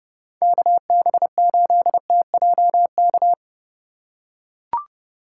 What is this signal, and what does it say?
Morse code, heard as a keyed tone, about 30 wpm, 700 Hz. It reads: KB8TJK
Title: USA Call Signs - 6 Characters - 2x Character Spacing - 30wpm